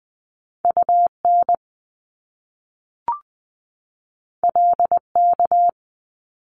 Morse code, heard as a keyed tone, about 20 wpm, 700 Hz. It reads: UN LK
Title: Top Two Letter Combinations in Words 20wpm